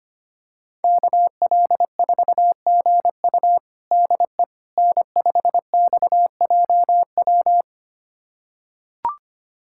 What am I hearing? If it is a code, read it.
Morse code, heard as a keyed tone, about 25 wpm, 700 Hz. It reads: KL4GU DE N5XJW
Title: USA Callsign DE Callsign - 5 Characters - 25wpm